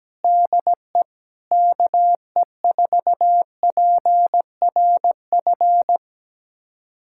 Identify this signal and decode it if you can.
Morse code, heard as a keyed tone, about 17 wpm, 700 Hz. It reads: DE KE4PRF
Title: USA Callsign DE Callsign - 6 Characters - 17wpm